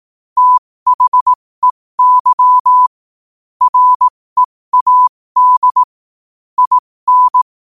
Morse code, heard as a keyed tone, about 18 wpm, spaced as for 15 wpm, 1000 Hz. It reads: THEY READ IN
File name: qotdfiles / 15wpm